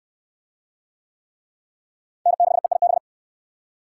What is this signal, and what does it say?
Morse code, heard as a keyed tone, about 62 wpm, 700 Hz. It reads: N6EIB